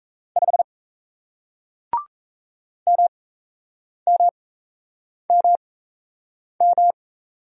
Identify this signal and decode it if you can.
Morse code, heard as a keyed tone, about 37 wpm, spaced as for 8 wpm, 700 Hz. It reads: 5 MMMM